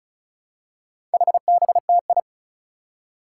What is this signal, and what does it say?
Morse code, heard as a keyed tone, about 36 wpm, 700 Hz. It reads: HBTI
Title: Four Challenging Letter-Number Combinations 35wpm